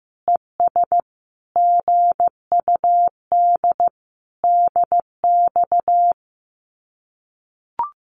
Morse code, heard as a keyed tone, about 15 wpm, 700 Hz. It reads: ES GUD DX
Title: Common QSO Phrases Encore 15wpm